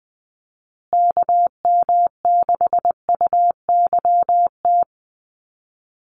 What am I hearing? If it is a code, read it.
Morse code, heard as a keyed tone, about 20 wpm, 700 Hz. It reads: KM6UYT
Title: USA Call Signs Repeated 2x 20wpm